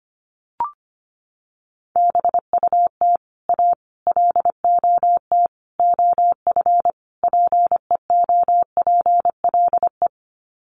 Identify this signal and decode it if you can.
Morse code, heard as a keyed tone, about 25 wpm, 700 Hz. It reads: BUT A LOT OF PEOPLE
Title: Sets of 5 Words Encore2 25wpm